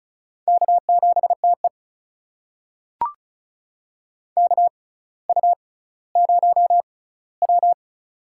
Morse code, heard as a keyed tone, about 34 wpm, spaced as for 18 wpm, 700 Hz. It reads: K7TE KU0W
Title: USA Call Signs - 4 Characters - 6x Character Spacing - 35wpm